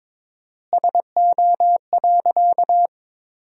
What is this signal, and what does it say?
Morse code, heard as a keyed tone, about 21 wpm, 700 Hz. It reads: SO.
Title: 4x Word-Spacing - Sentences from Top 200 Words - 22wpm